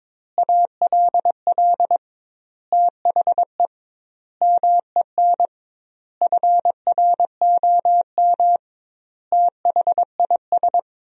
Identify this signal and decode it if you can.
Morse code, heard as a keyed tone, about 21 wpm, 700 Hz. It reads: ALL THE MEN FROM THIS
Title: 2x Word-Spacing - Sentences from Top 300 Words - 22wpm